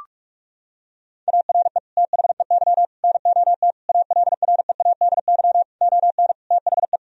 Morse code, heard as a keyed tone, about 46 wpm, 700 Hz. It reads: ARE THEY NOT ALREADY ON THE
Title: Sentences from Top 1000 Words - Review Part 1 - 45wpm